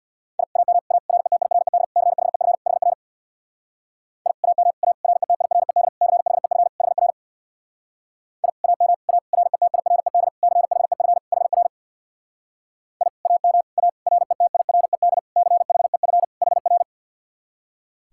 Morse code, heard as a keyed tone, about 65 wpm, 700 Hz. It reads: I AM A RETIRED CHEF HR I AM A RETIRED CHEF HR I AM A RETIRED CHEF HR I AM A RETIRED CHEF HR